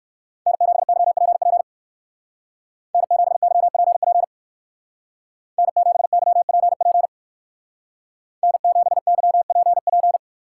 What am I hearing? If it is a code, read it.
Morse code, heard as a keyed tone, about 56 wpm, 700 Hz. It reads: N7YPP N7YPP N7YPP N7YPP